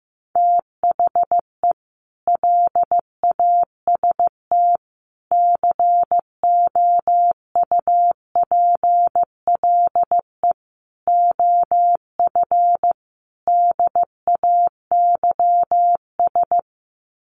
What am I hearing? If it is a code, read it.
Morse code, heard as a keyed tone, about 14 wpm, 700 Hz. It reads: THE LAST COUPLE OF DAYS
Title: Sets of 5 Words Encore 15wpm